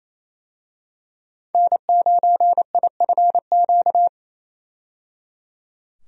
Morse code, heard as a keyed tone, about 28 wpm, 700 Hz. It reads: N9IFQ
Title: USA Call Signs - 5 Characters - 28wpm